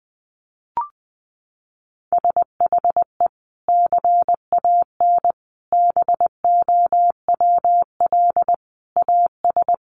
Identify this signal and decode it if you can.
Morse code, heard as a keyed tone, about 20 wpm, 700 Hz. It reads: SHE CAN BOWL AS
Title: Sentences from Top 1500 Words 20wpm